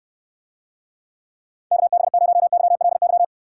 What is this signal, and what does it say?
Morse code, heard as a keyed tone, about 69 wpm, 700 Hz. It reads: KB0YLY